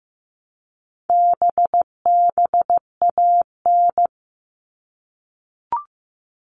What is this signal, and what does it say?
Morse code, heard as a keyed tone, about 15 wpm, 700 Hz. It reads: BBAN
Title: All Cut Numbers - 4-Digit 15wpm